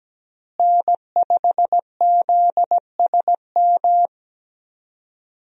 Morse code, heard as a keyed tone, about 17 wpm, 700 Hz. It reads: N5ZSM
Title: USA Call Signs - 5 Characters - Repeated 3x - 17wpm